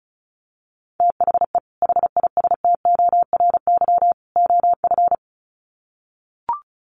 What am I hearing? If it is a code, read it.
Morse code, heard as a keyed tone, about 34 wpm, 700 Hz. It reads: THE HISTORY OF
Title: Sets of 3 Words 35wpm